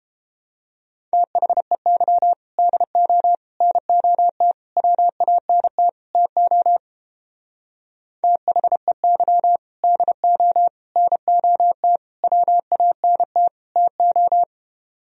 Morse code, heard as a keyed tone, about 29 wpm, 700 Hz. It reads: THEY DO NOT WANT TO THEY DO NOT WANT TO